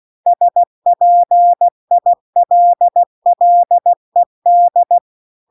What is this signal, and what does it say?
Morse code, heard as a keyed tone, about 16 wpm, 700 Hz. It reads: SPILLED